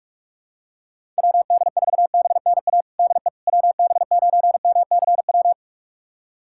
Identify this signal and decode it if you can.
Morse code, heard as a keyed tone, about 46 wpm, 690 Hz. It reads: WD4BNA DE WB9MCW
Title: USA Callsign DE Callsign - 6 Characters - 45wpm